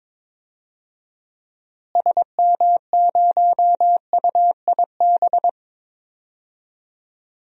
Morse code, heard as a keyed tone, about 21 wpm, 700 Hz. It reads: SM0UIB
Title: DXCC-100 Call Signs - 6 Characters - Repeated 3x - 22wpm